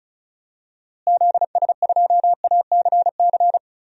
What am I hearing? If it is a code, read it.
Morse code, heard as a keyed tone, about 36 wpm, 700 Hz. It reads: ZS2ACC
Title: DXCC-100 Call Signs - 6 Characters - Repeated 3x - 35wpm